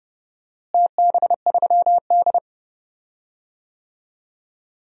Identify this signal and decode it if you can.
Morse code, heard as a keyed tone, about 32 wpm, 700 Hz. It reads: TB3D
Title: DXCC-100 Call Signs - 4 Characters - Repeated 2x - 30wpm